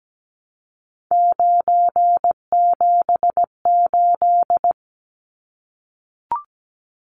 Morse code, heard as a keyed tone, about 17 wpm, 700 Hz. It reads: 978
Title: Numbers - 3-Digit 17wpm